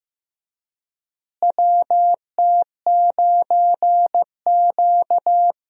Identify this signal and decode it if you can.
Morse code, heard as a keyed tone, about 15 wpm, 700 Hz. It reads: WT9Q
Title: USA Call Signs - 4 Characters - Repeated 3x - 15wpm